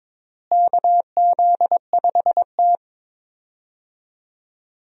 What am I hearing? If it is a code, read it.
Morse code, heard as a keyed tone, about 22 wpm, 700 Hz. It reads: KZ5T
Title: USA Call Signs - 4 Characters - Repeated 2x - 22wpm